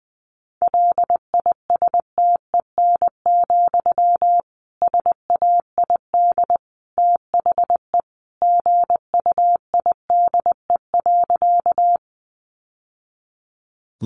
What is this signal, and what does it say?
Morse code, heard as a keyed tone, about 20 wpm, 700 Hz. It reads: LISTEN, SAID THE GUIDE.
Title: Sentences from Top 600 Words 20wpm